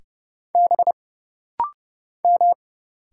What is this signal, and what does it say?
Morse code, heard as a keyed tone, about 30 wpm, 700 Hz. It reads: B M
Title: Single Letter-Number - Rapid-Fire 30wpm